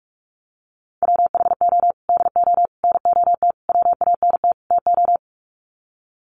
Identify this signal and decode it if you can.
Morse code, heard as a keyed tone, about 44 wpm, 700 Hz. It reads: WHO DO NOT WANT TO
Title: Sets of 5 Words Encore2 45wpm